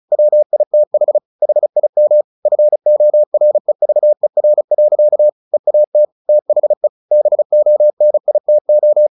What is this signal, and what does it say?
Morse code, heard as a keyed tone, about 35 wpm, 600 Hz. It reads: WITH HIM FOREVER. EAT THE BONITO
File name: the old man and the sea / the old man and the sea 1 35wpm